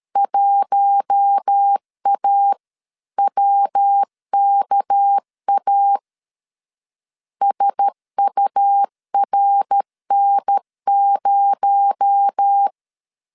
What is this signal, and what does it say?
Morse code, heard as a keyed tone, about 13 wpm, 800 Hz. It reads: 1AWKA SURN0
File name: CW / DET / 1 6 1500C S A 12wpm 3440